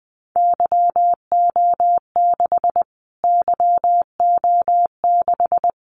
Morse code, heard as a keyed tone, about 20 wpm, 700 Hz. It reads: YO6 YO6
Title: Callsign Prefixes for Top 25 DXCC - Repeated 3x - 20wpm